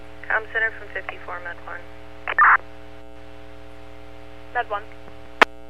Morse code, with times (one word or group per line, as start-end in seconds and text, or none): none